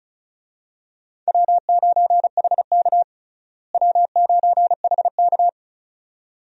Morse code, heard as a keyed tone, about 36 wpm, 700 Hz. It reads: W9HK W9HK
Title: USA Call Signs - 4 Characters - Repeated 2x - 35wpm